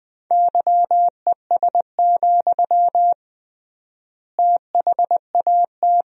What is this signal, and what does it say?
Morse code, heard as a keyed tone, about 20 wpm, 700 Hz. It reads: YES, THAT
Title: 3x Word-Spacing - Sentences from Top 500 Words - 20wpm